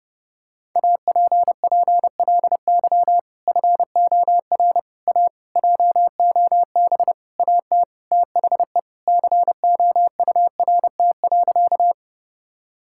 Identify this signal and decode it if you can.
Morse code, heard as a keyed tone, about 32 wpm, 700 Hz. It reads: APPLY FOR A JOB AT THE COURT.
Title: Sentences from Top 1600 Words 30wpm